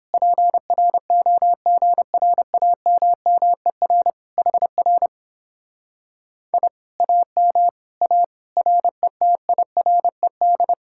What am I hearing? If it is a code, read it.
Morse code, heard as a keyed tone, about 29 wpm, 700 Hz. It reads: PROGRAMMER HR I AM A RETIRED